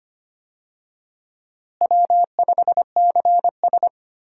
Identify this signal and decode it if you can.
Morse code, heard as a keyed tone, about 25 wpm, 700 Hz. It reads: W5CS
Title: USA Call Signs - 4 Characters - Repeated 2x - 25wpm